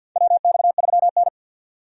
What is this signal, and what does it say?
Morse code, heard as a keyed tone, about 51 wpm, 700 Hz. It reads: WX3N
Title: USA Call Signs - 4 Characters - 50wpm